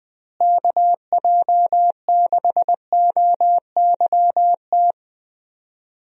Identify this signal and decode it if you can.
Morse code, heard as a keyed tone, about 20 wpm, 700 Hz. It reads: KJ6OYT
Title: USA Call Signs - 6 Characters - Repeated 2x - 20wpm